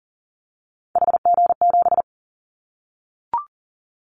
Morse code, heard as a keyed tone, about 40 wpm, 700 Hz. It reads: HG7
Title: Callsign Prefixes for Top 25 DXCC - 40wpm